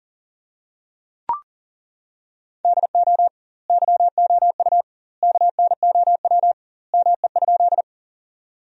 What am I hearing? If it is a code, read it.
Morse code, heard as a keyed tone, about 40 wpm, 700 Hz. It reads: DO YOU KNOW ME?
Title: 2x Word-Spacing - Sentences from Top 200 Words - 40wpm